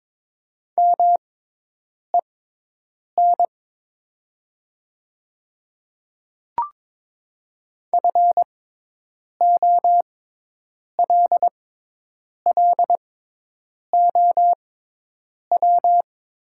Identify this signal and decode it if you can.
Morse code, heard as a keyed tone, about 22 wpm, spaced as for 8 wpm, 700 Hz. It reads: MEN FOLLOW
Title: 6x Character Spacing - Top 300 Words - 22wpm